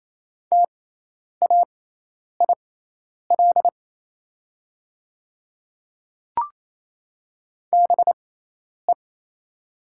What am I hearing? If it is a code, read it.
Morse code, heard as a keyed tone, about 28 wpm, spaced as for 10 wpm, 700 Hz. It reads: TAIL BE
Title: 6x Character Spacing - Top 500 Words - 28wpm